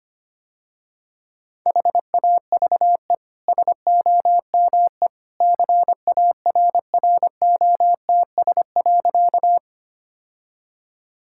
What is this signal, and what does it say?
Morse code, heard as a keyed tone, about 25 wpm, 690 Hz. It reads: HAVE SOME CARROTS.
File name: Easy Sentences with Hints - 25wpm